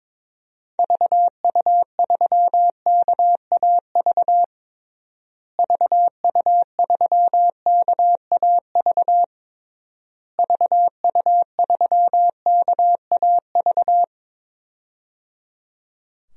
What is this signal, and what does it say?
Morse code, heard as a keyed tone, about 22 wpm, 700 Hz. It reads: VU3KAV VU3KAV VU3KAV
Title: DXCC-100 Call Signs - 6 Characters - Repeated 3x - 22wpm